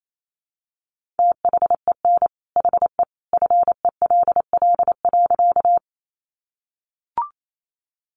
Morse code, heard as a keyed tone, about 28 wpm, 700 Hz. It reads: THEN HE FELL.
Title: A Sound Of Thunder 28wpm